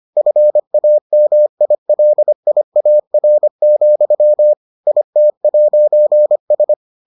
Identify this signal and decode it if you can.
Morse code, heard as a keyed tone, about 25 wpm, 600 Hz. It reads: FAMILIAR, IT'S